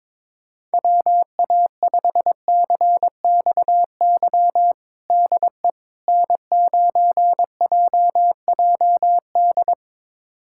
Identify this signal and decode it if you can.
Morse code, heard as a keyed tone, about 21 wpm, 700 Hz. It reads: WA5CXY DE N9JJD
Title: USA Callsign DE Callsign 22wpm